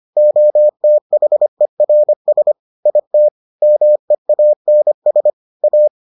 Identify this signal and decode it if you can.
Morse code, heard as a keyed tone, about 25 wpm, 600 Hz. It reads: OTHERS IT MEANS A